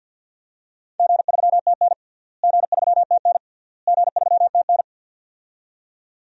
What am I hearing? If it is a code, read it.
Morse code, heard as a keyed tone, about 52 wpm, 700 Hz. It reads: G3TN G3TN G3TN
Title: DXCC-100 Call Signs - 4 Characters - Repeated 3x - 50wpm